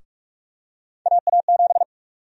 Morse code, heard as a keyed tone, about 46 wpm, 700 Hz. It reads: AA7
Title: USA Call Sign Prefixes - Repeated 2x - 45wpm